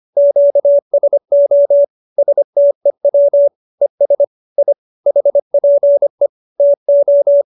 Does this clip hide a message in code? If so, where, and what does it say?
Yes: Morse code from 0.2 to 7.4 s, heard as a keyed tone, about 25 wpm, 570 Hz. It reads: QSO STEW ES I HPE TO